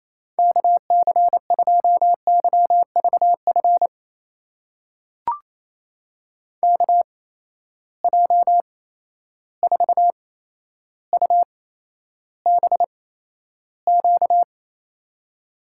Morse code, heard as a keyed tone, about 27 wpm, spaced as for 12 wpm, 700 Hz. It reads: KC2YVF KJ4UBQ
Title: USA Call Signs - 6 Characters - 8x Character Spacing - 28wpm